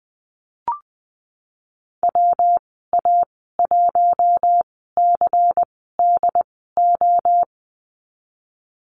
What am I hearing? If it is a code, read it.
Morse code, heard as a keyed tone, about 20 wpm, spaced as for 15 wpm, 700 Hz. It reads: WA1CDO
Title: 122 Lesson 20g - Number 3 - Call Signs 20wpm